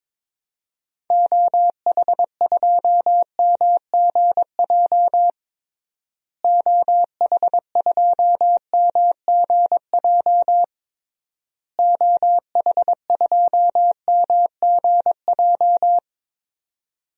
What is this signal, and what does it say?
Morse code, heard as a keyed tone, about 22 wpm, 700 Hz. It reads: OH2MGJ OH2MGJ OH2MGJ
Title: DXCC-100 Call Signs - 6 Characters - Repeated 3x - 22wpm